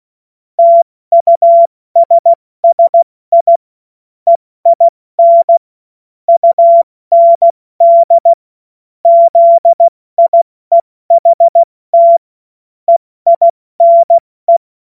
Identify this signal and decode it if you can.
Morse code, heard as a keyed tone, about 16 wpm, spaced as for 14 wpm, 680 Hz. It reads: TUSSI EIN UND ZIEHT EINE
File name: Morsekurs / Witze / Witz 31. August 2021, 70 F80